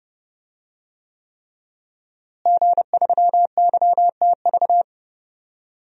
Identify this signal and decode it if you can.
Morse code, heard as a keyed tone, about 31 wpm, 700 Hz. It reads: G3YTV
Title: DXCC-100 Call Signs - 5 Characters - Repeated 3x - 30wpm